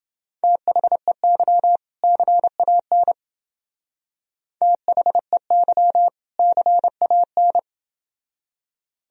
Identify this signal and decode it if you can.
Morse code, heard as a keyed tone, about 28 wpm, 700 Hz. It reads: THEY CAN THEY CAN